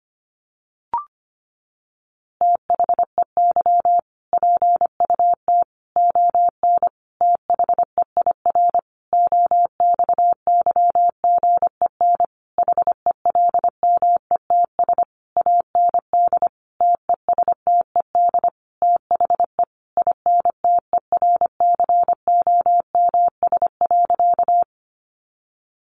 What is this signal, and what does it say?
Morse code, heard as a keyed tone, about 25 wpm, 700 Hz. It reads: THEY PUT ON THEIR OXYGEN HELMETS AND TESTED THE INTERCOMS.
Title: A Sound Of Thunder 25wpm